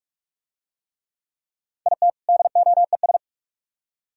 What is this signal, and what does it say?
Morse code, heard as a keyed tone, about 41 wpm, 700 Hz. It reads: IT DOES